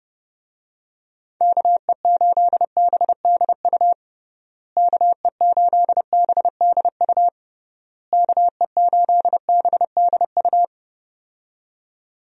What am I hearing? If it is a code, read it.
Morse code, heard as a keyed tone, about 30 wpm, 700 Hz. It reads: KE8BDU KE8BDU KE8BDU
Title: USA Call Signs - 6 Characters - Repeated 3x - 30wpm